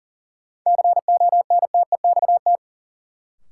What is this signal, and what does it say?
Morse code, heard as a keyed tone, about 39 wpm, 700 Hz. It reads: CONTEXT
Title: Sentences from Top 1000 Words - Review Part 7 - New Word Drill - 40wpm